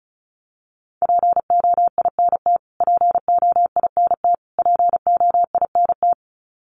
Morse code, heard as a keyed tone, about 34 wpm, 700 Hz. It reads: POINT POINT POINT
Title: Top 200 Words Repeated 35wpm